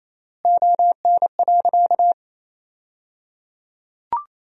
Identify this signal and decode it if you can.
Morse code, heard as a keyed tone, about 28 wpm, 700 Hz. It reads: ON.
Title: 3x Word-Spacing - Sentences from Top 200 Words - 28wpm